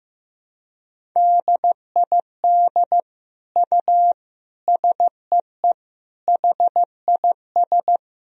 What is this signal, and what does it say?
Morse code, heard as a keyed tone, about 15 wpm, 700 Hz. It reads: DID U SEE HIS
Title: Sentences from Top 300 Words and Abbreviations 15wpm